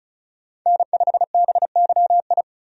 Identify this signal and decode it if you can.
Morse code, heard as a keyed tone, about 34 wpm, 700 Hz. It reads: N5BYI
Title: USA Call Signs - 5 Characters - 35wpm